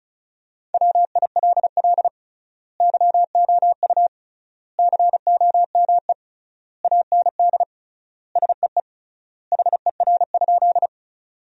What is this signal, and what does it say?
Morse code, heard as a keyed tone, about 36 wpm, 700 Hz. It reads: WILL YOU COME AND SEE HER?
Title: 3x Word-Spacing - Sentences from Top 100 Words - 35wpm